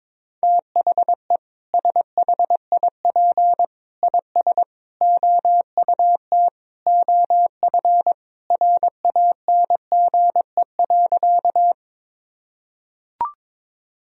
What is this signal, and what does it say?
Morse code, heard as a keyed tone, about 22 wpm, 700 Hz. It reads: THE SHIP IS OUT OF RANGE.
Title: Sentences from Top 1000 Words - Review Part 4 - 22wpm